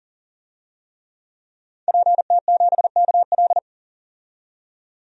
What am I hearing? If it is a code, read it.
Morse code, heard as a keyed tone, about 40 wpm, 700 Hz. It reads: PT7KL
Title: DXCC-100 Call Signs - 5 Characters - 40wpm